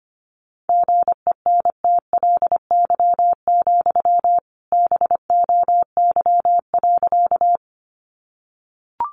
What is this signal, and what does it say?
Morse code, heard as a keyed tone, about 25 wpm, 700 Hz. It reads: GENTLY, BOY.
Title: Sentences from Top 1500 Words 25wpm